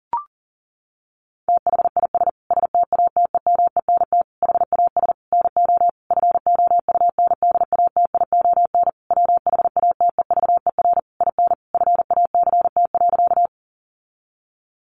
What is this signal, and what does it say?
Morse code, heard as a keyed tone, about 37 wpm, 700 Hz. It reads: THIS STATEMENT HAS NO FOUNDATION WHATEVER IN FACT.
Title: Sentences from Top 1200 Words 40wpm